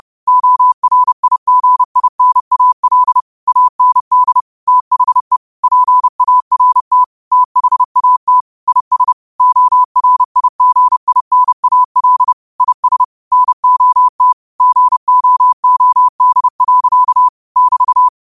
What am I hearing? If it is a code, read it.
Morse code, heard as a keyed tone, about 30 wpm, 1000 Hz. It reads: ORIGINAL AND THE PART THAT IS ORIGINAL IS NOT GOOD. <BT>
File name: qotdfiles / 30wpm